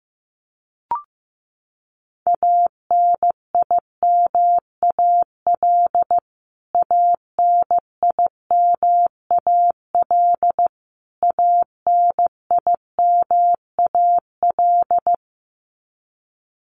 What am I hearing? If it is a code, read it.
Morse code, heard as a keyed tone, about 15 wpm, 700 Hz. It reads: ANIMAL ANIMAL ANIMAL
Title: Top 200 Words Repeated 15wpm